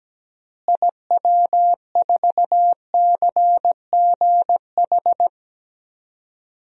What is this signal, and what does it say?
Morse code, heard as a keyed tone, about 17 wpm, 700 Hz. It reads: IW4CGH
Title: DXCC-100 Call Signs - 6 Characters - Repeated 3x - 17wpm